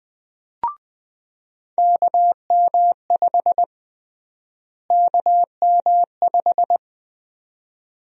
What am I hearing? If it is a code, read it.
Morse code, heard as a keyed tone, about 20 wpm, 700 Hz. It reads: KM5 KM5
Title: USA Call Sign Prefixes - Repeated 2x - 20wpm